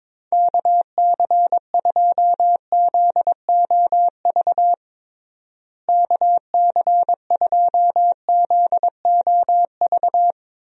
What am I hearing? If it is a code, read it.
Morse code, heard as a keyed tone, about 22 wpm, 700 Hz. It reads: KC2ZOV KC2ZOV
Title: USA Call Signs - 6 Characters - Repeated 2x - 22wpm